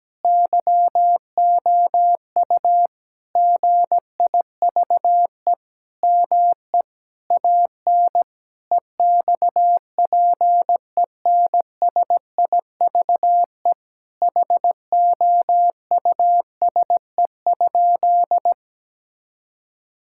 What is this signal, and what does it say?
Morse code, heard as a keyed tone, about 17 wpm, 700 Hz. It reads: YOU GIVE ME AN EXPENSIVE HOUSE?
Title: Sentences from Top 1000 Words - Review Part 6 - 17wpm